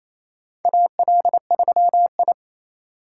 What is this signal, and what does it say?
Morse code, heard as a keyed tone, about 27 wpm, 700 Hz. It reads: AL3I
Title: USA Call Signs - 4 Characters - 3x Character Spacing - 28wpm